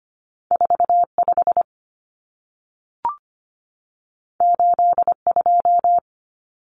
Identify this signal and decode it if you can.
Morse code, heard as a keyed tone, about 25 wpm, 700 Hz. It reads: 45 82
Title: Common Cut Numbers - 2-Digit 25wpm